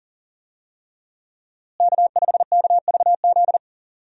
Morse code, heard as a keyed tone, about 40 wpm, 700 Hz. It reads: K5KVZ
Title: USA Call Signs - 5 Characters - Repeated 3x - 40wpm